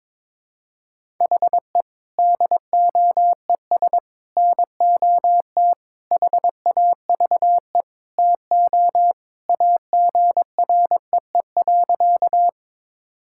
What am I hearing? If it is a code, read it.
Morse code, heard as a keyed tone, about 22 wpm, 700 Hz. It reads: HE DOES NOT HAVE TO AGREE.
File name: practice / 22wpm / Sentences with 4-Word Phrases - 22wpm